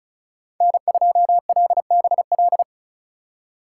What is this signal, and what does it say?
Morse code, heard as a keyed tone, about 36 wpm, 700 Hz. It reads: N2LBL
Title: USA Call Signs - 5 Characters - 3x Character Spacing - 35wpm